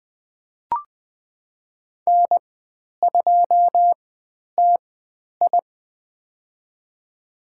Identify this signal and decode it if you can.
Morse code, heard as a keyed tone, about 20 wpm, spaced as for 10 wpm, 700 Hz. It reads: N2TI
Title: 109 Lesson 18g - Letter m - Call Signs 20wpm